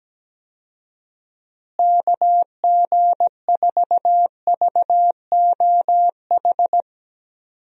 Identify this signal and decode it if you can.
Morse code, heard as a keyed tone, about 17 wpm, 700 Hz. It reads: KG4VOH
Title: USA Call Signs - 6 Characters - Repeated 3x - 17wpm